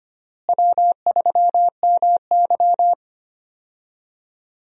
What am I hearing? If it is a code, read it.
Morse code, heard as a keyed tone, about 25 wpm, 700 Hz. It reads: W3MY
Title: USA Call Signs - 4 Characters - Repeated 3x - 25wpm